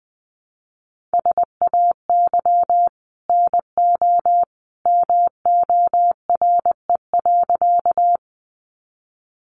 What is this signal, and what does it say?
Morse code, heard as a keyed tone, about 20 wpm, 700 Hz. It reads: SAY NO MORE.
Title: Sentences from Top 200 Words 20wpm